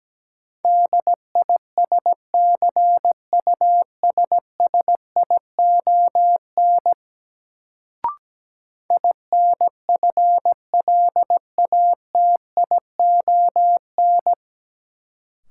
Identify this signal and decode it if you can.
Morse code, heard as a keyed tone, about 17 wpm, 700 Hz. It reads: DISCUSSION INFLATION
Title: Sentences from Top 1100 Words - New Word Drill - 17wpm